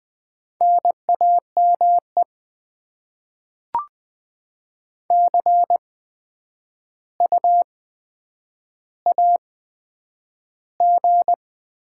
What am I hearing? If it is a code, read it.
Morse code, heard as a keyed tone, about 20 wpm, 700 Hz. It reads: N A M E C U A G
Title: Top 20 QSO Elements - 8x Character Spacing - 20wpm